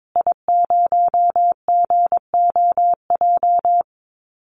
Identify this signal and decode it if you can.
Morse code, heard as a keyed tone, about 22 wpm, 700 Hz. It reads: I0GOJ
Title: Call Signs Repeated 2x - 22wpm